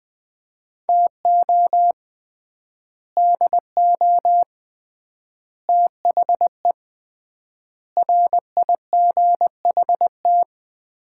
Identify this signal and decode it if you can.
Morse code, heard as a keyed tone, about 20 wpm, 700 Hz. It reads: TO DO THE RIGHT
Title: 3x Word-Spacing - Sets of 5 Words - 20wpm